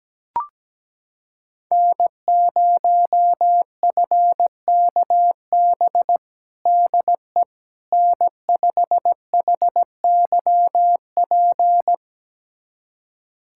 Morse code, heard as a keyed tone, about 17 wpm, 700 Hz. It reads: N0FKB DE N5HYP
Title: USA Callsign DE Callsign - 5 Characters - 17wpm